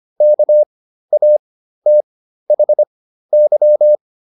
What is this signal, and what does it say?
Morse code, heard as a keyed tone, about 25 wpm, spaced as for 13 wpm, 600 Hz. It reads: KATHY